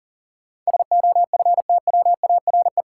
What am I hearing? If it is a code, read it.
Morse code, heard as a keyed tone, about 37 wpm, 700 Hz. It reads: SOFTWARE
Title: Sentences from Top 1000 Words - Review Part 3 - New Word Drill - 40wpm